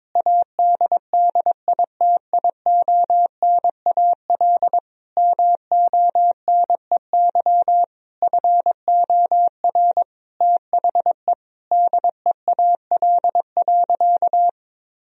Morse code, heard as a keyed tone, about 22 wpm, 700 Hz. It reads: ADDITIONAL MONEY FOR THE DEAL.
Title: Sentences from Top 1000 Words - Review Part 4 - 22wpm